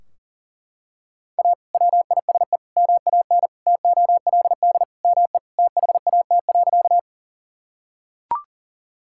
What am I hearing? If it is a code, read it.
Morse code, heard as a keyed tone, about 40 wpm, 700 Hz. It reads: A WISE MAN TOLD ME THAT.
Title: Sentences from Top 1600 Words 40wpm